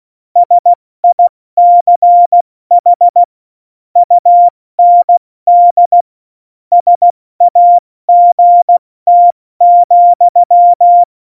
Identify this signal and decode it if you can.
Morse code, heard as a keyed tone, about 16 wpm, 700 Hz. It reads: SICH UND SAGT,
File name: Morsekurs / Witze / Witz 26. November 2019, 70 F80